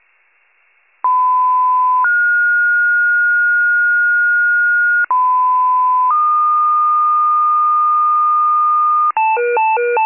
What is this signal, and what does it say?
Two-tone page, heard as a one-second tone, then a three-second tone, about 1010 then 1430 Hz, about 1010 then 1220 Hz, with noise. It is heard from 1.0 to 9.1 s.